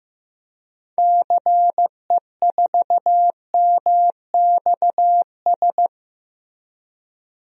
Morse code, heard as a keyed tone, about 15 wpm, 700 Hz. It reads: CE4MXS
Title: DXCC-100 Call Signs - 6 Characters - Repeated 3x - 15wpm